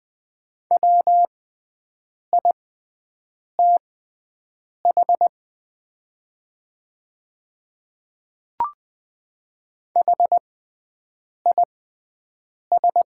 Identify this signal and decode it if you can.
Morse code, heard as a keyed tone, about 20 wpm, spaced as for 7 wpm, 700 Hz. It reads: WITH HIS
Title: 6x Character Spacing - Top 100 Words - 20wpm